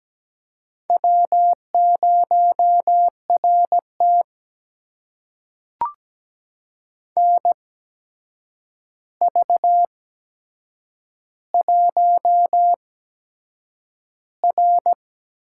Morse code, heard as a keyed tone, about 17 wpm, spaced as for 7 wpm, 700 Hz. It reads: W0RT NV1R
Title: USA Call Signs - 4 Characters - 8x Character Spacing - 17wpm